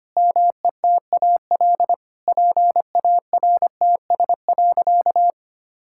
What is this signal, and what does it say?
Morse code, heard as a keyed tone, about 25 wpm, 700 Hz. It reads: METAL PARTS.
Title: Sentences from Top 1000 Words - Review Part 6 - 25wpm